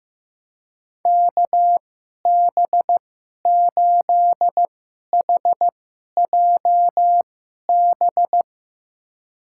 Morse code, heard as a keyed tone, about 15 wpm, spaced as for 11 wpm, 700 Hz. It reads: KB8HJB
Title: USA Call Signs - 6 Characters - 2x Character Spacing - 15wpm